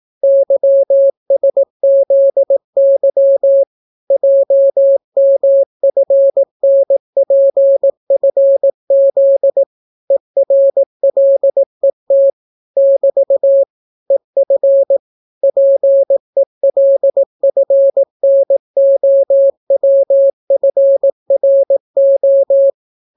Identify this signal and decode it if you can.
Morse code, heard as a keyed tone, about 18 wpm, 550 Hz. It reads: YSZY JMFNPFZ ERLET <BT> EF PELFNOWFRO